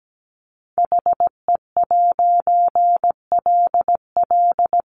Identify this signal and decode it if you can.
Morse code, heard as a keyed tone, about 17 wpm, 700 Hz. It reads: HE'LL
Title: A Sound Of Thunder 17wpm